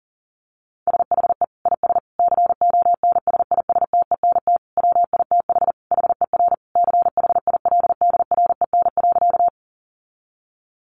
Morse code, heard as a keyed tone, about 40 wpm, 700 Hz. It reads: SHE IS CONSISTENT WITH HER CHILDREN.
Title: Sentences from Top 1500 Words 40wpm